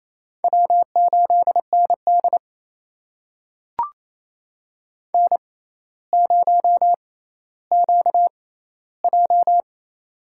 Morse code, heard as a keyed tone, about 28 wpm, spaced as for 14 wpm, 700 Hz. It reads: W8ND N0QJ
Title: USA Call Signs - 4 Characters - 6x Character Spacing - 28wpm